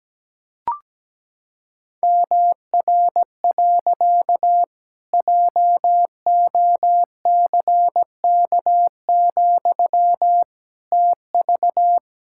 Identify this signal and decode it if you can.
Morse code, heard as a keyed tone, about 17 wpm, 700 Hz. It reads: MR. JOCK, TV
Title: Pangram Challenge v2 - 17wpm